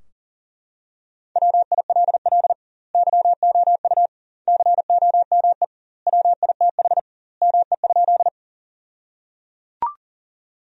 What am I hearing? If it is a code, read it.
Morse code, heard as a keyed tone, about 37 wpm, 700 Hz. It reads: WILL YOU COME WITH ME?
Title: 2x Word-Spacing - Sentences from Top 200 Words - 40wpm